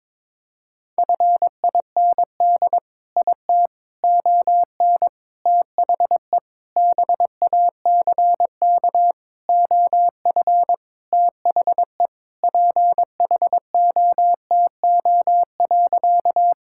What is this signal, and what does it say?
Morse code, heard as a keyed tone, about 22 wpm, 700 Hz. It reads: FIND IT ON THE BACK OF THE PHOTO.
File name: practice / 22wpm / Sentences with 4-Word Phrases - 22wpm